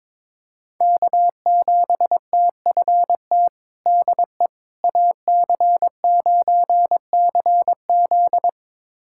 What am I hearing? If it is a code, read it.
Morse code, heard as a keyed tone, about 22 wpm, 700 Hz. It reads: K7TFT DE AC9CZ
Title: USA Callsign DE Callsign - 5 Characters - 22wpm